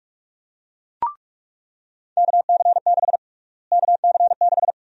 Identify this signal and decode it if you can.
Morse code, heard as a keyed tone, about 47 wpm, 700 Hz. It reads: KC6 KC6
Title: USA Call Sign Prefixes - Repeated 2x - 45wpm